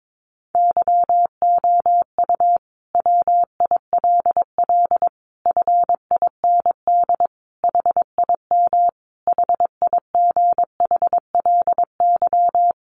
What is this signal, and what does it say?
Morse code, heard as a keyed tone, about 22 wpm, 700 Hz. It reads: YOU WILL FIND HIM HIGHLY
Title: Sentences from Top 1100 Words 22wpm